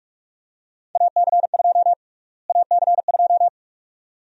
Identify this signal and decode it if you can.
Morse code, heard as a keyed tone, about 46 wpm, 700 Hz. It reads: AC2 AC2
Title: USA Call Sign Prefixes - Repeated 2x - 45wpm